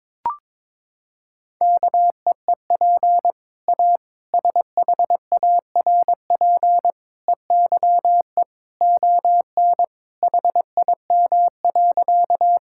Morse code, heard as a keyed tone, about 22 wpm, 700 Hz. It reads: KEEP A SHARP EYE ON HIM.
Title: Sentences from Top 1600 Words 22wpm